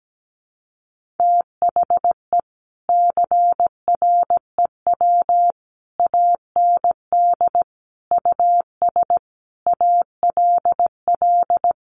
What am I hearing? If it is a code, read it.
Morse code, heard as a keyed tone, about 17 wpm, 700 Hz. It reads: THE CREW AND US ALL
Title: Sentences from Top 1500 Words 17wpm